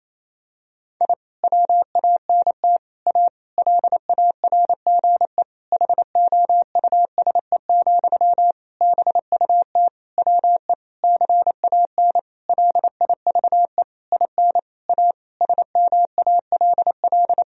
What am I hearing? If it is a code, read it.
Morse code, heard as a keyed tone, about 28 wpm, 700 Hz. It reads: I WANT A LARGE HOUSE, BUT WE CAN LIVE IN A SMALL
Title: Sentences from Top 1000 Words - Review Part 1 - 28wpm